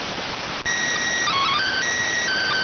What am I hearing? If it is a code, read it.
SSTV (slow-scan television), PD 290.